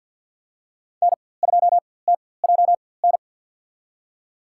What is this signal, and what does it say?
Morse code, heard as a keyed tone, about 51 wpm, spaced as for 23 wpm, 700 Hz. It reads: N2TJN
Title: USA Call Signs - 5 Characters - 4x Character Spacing - 50wpm